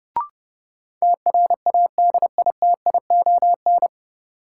Morse code, heard as a keyed tone, about 30 wpm, 700 Hz. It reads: TRADITION
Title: Sentences from Top 1500 Words - New Word Drill Alt - 30wpm